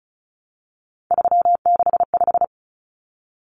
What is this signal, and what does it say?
Morse code, heard as a keyed tone, about 34 wpm, 700 Hz. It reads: 365
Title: Common Cut Numbers - 3-Digit 35wpm